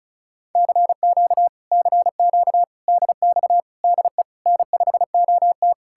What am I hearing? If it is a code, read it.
Morse code, heard as a keyed tone, about 47 wpm, spaced as for 41 wpm, 700 Hz. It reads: CQ CQ DX DE N5OT
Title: Speed-Racing Common QSO Phrases Encore - 35wpm